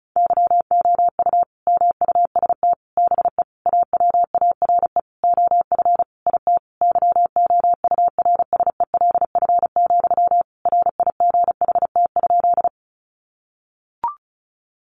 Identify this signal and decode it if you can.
Morse code, heard as a keyed tone, about 34 wpm, 700 Hz. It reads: YOU MUST BE AWARE OF IT YOURSELF, RIGHT?
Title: Sentences from Top 1200 Words 35wpm